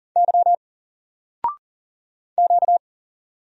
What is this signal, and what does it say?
Morse code, heard as a keyed tone, about 39 wpm, 700 Hz. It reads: Y Q
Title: Single Letter-Number 40wpm